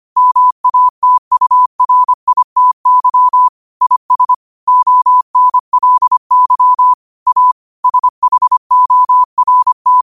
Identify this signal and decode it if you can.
Morse code, heard as a keyed tone, about 25 wpm, 1000 Hz. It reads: MATURITY IS ONLY A SHORT
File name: qotdfiles / 25wpm